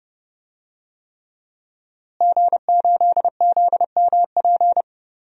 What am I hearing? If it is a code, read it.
Morse code, heard as a keyed tone, about 30 wpm, 700 Hz. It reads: G8ZMP
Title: DXCC-100 Call Signs - 5 Characters - Repeated 3x - 30wpm